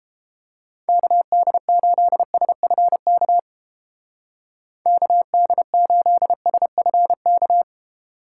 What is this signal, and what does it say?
Morse code, heard as a keyed tone, about 31 wpm, 700 Hz. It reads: KD8SFK KD8SFK